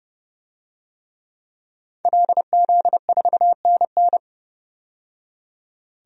Morse code, heard as a keyed tone, about 30 wpm, 700 Hz. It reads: LZ4NN
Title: DXCC-100 Call Signs - 5 Characters - 30wpm